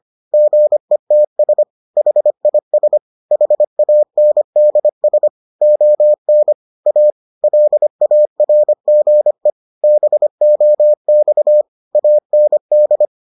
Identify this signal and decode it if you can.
Morse code, heard as a keyed tone, about 25 wpm, 600 Hz. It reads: GETS HIS HANDS ON A LARGE BOX AND